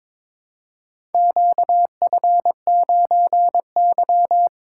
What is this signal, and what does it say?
Morse code, heard as a keyed tone, about 22 wpm, 700 Hz. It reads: QF9Y
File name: practice / 22wpm / Four Letter-Number Combinations - 22wpm